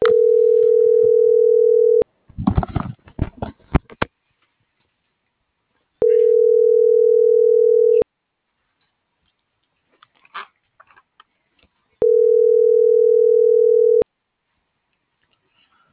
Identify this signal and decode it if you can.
Ringback tone (North America), 440 + 480 Hz.